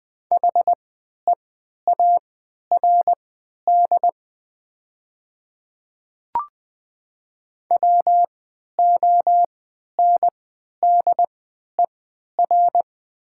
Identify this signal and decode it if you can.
Morse code, heard as a keyed tone, about 20 wpm, spaced as for 11 wpm, 700 Hz. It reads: HEARD WONDER
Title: 3x Character Spacing - Top 500 Words - 20wpm